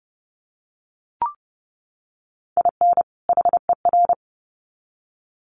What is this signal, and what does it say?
Morse code, heard as a keyed tone, about 30 wpm, 700 Hz. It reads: IN HER
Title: Sets of 2 Words 30wpm